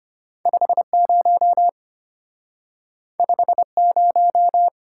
Morse code, heard as a keyed tone, about 34 wpm, 700 Hz. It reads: 50 50